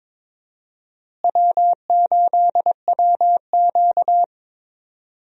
Morse code, heard as a keyed tone, about 22 wpm, 700 Hz. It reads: W8WQ
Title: USA Call Signs - 4 Characters - Repeated 2x - 22wpm